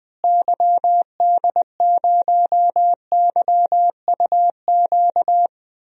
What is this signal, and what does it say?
Morse code, heard as a keyed tone, about 20 wpm, 700 Hz. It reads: YD0YUQ
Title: DXCC-100 Call Signs - 6 Characters - Repeated 2x - 20wpm